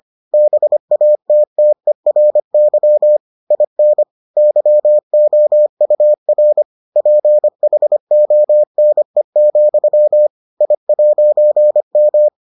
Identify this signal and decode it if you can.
Morse code, heard as a keyed tone, about 25 wpm, 600 Hz. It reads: BATTERY IN YOUR PHONE, I'M